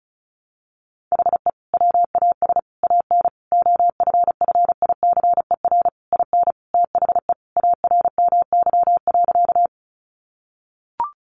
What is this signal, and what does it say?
Morse code, heard as a keyed tone, about 34 wpm, 700 Hz. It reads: HE WAS AN OFFICER IN THE ARMY.
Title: Sentences from Top 1200 Words 35wpm